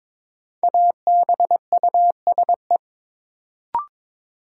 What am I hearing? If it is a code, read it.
Morse code, heard as a keyed tone, about 22 wpm, 700 Hz. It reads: ABUSE
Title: Sentences from Top 1300 Words - New Word Drill - 22wpm